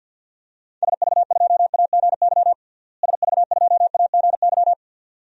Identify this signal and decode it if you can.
Morse code, heard as a keyed tone, about 53 wpm, 700 Hz. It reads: SV2AGY SV2AGY
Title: DXCC-100 Call Signs - 6 Characters - Repeated 2x - 50wpm